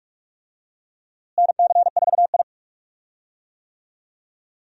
Morse code, heard as a keyed tone, about 44 wpm, 700 Hz. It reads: NC4I
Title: USA Call Signs - 4 Characters - Repeated 2x - 45wpm